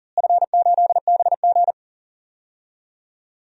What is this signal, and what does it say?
Morse code, heard as a keyed tone, about 40 wpm, 700 Hz. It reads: F8BG
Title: DXCC-100 Call Signs - 4 Characters - Repeated 2x - 40wpm